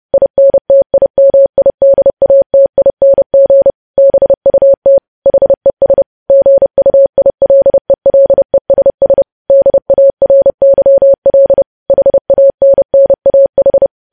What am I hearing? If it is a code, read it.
Morse code, heard as a keyed tone, about 30 wpm, 550 Hz. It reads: INTIMIDATING BUT HES GUILELESS DARYL HANNAH
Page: news headlines in morse code at 30 wpm